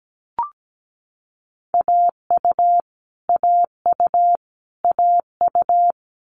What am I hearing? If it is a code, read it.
Morse code, heard as a keyed tone, about 17 wpm, 700 Hz. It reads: AU AU AU
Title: Multi-letter Phonemes Repeated 3x 17wpm